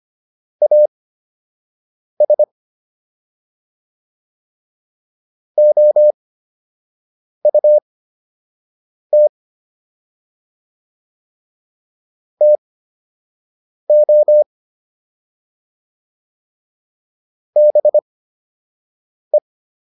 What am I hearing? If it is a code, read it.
Morse code, heard as a keyed tone, about 25 wpm, spaced as for 6 wpm, 600 Hz. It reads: AS OUT TO BE